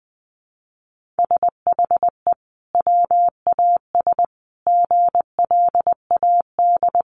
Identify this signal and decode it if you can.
Morse code, heard as a keyed tone, about 19 wpm, 700 Hz. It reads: SHE WAS GLAD
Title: Sentences from Top 800 Words 20wpm